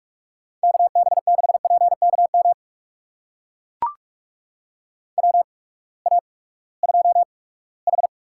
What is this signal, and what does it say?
Morse code, heard as a keyed tone, about 41 wpm, spaced as for 22 wpm, 700 Hz. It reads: KB6PKM WA2H
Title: USA Call Signs - 6 Characters - 8x Character Spacing - 45wpm